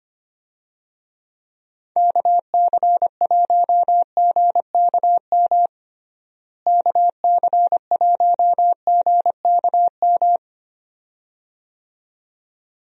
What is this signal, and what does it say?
Morse code, heard as a keyed tone, about 25 wpm, 700 Hz. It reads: KC1GKM KC1GKM
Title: USA Call Signs - 6 Characters - Repeated 2x - 25wpm